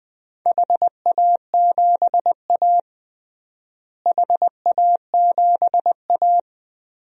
Morse code, heard as a keyed tone, about 19 wpm, 700 Hz. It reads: HA7A HA7A
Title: DXCC-100 Call Signs - 4 Characters - Repeated 3x - 20wpm